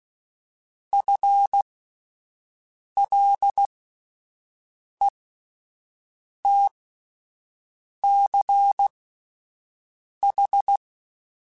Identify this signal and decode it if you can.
Morse code, heard as a keyed tone, about 16 wpm, spaced as for 5 wpm, 800 Hz. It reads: FLETCH
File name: Alphabetical / Alphabetical mp3s / Fs @5